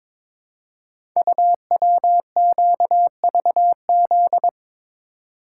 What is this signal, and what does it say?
Morse code, heard as a keyed tone, about 22 wpm, 700 Hz. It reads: UWQVZ
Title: Code Groups - 5 Letters - 22wpm